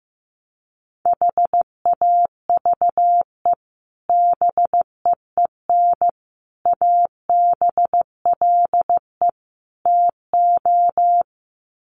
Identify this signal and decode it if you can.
Morse code, heard as a keyed tone, about 15 wpm, 700 Hz. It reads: HAVE BEEN ABLE TO
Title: Sets of 4 Words 15wpm